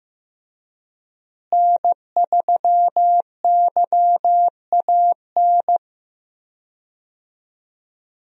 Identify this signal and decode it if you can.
Morse code, heard as a keyed tone, about 15 wpm, 700 Hz. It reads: N3YAN
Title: USA Call Signs - 5 Characters - Repeated 3x - 15wpm